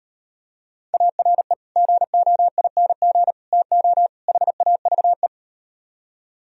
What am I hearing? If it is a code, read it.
Morse code, heard as a keyed tone, about 38 wpm, 700 Hz. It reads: ARE GOING TO HAVE